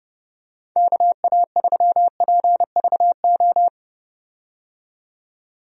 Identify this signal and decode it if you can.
Morse code, heard as a keyed tone, about 30 wpm, 700 Hz. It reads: KA3PVO
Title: USA Call Signs - 6 Characters - 30wpm